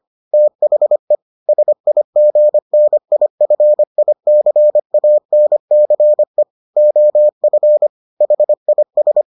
Morse code, heard as a keyed tone, about 25 wpm, 600 Hz. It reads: THE SIGNIFICANCE OF HIS